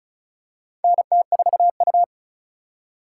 Morse code, heard as a keyed tone, about 35 wpm, 700 Hz. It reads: NT4U
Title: USA Call Signs - 4 Characters - 2x Character Spacing - 35wpm